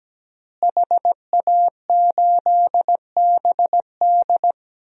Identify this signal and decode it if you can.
Morse code, heard as a keyed tone, about 17 wpm, 700 Hz. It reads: HA8BD
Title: DXCC-100 Call Signs - 5 Characters - Repeated 2x - 17wpm